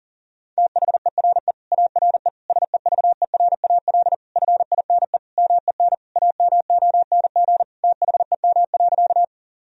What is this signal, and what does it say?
Morse code, heard as a keyed tone, about 40 wpm, 700 Hz. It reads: THERE ARE SEVERAL FINE MEN AMONG THEM.
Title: Sentences from Top 1000 Words - Review Part 2 - 40wpm